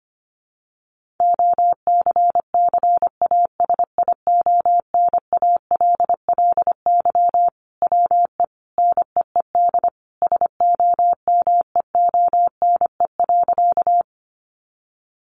Morse code, heard as a keyed tone, about 25 wpm, 700 Hz. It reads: OCCASIONALLY WE NEED SOMEONE.
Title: Sentences from Top 1300 Words 25wpm